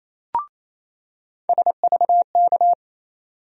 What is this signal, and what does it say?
Morse code, heard as a keyed tone, about 28 wpm, 700 Hz. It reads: SVK
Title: USA Call Sign Postfixes - 3 Letters 28wpm